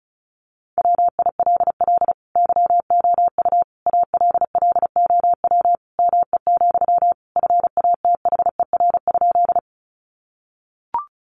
Morse code, heard as a keyed tone, about 34 wpm, 700 Hz. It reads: WILL YOU ALLOW ME, FATHER?
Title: Sentences from Top 700 Words 35wpm